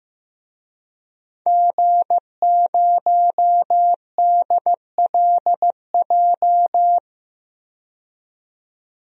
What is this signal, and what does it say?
Morse code, heard as a keyed tone, about 15 wpm, 700 Hz. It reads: G0DLJ
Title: DXCC-100 Call Signs - 5 Characters - 15wpm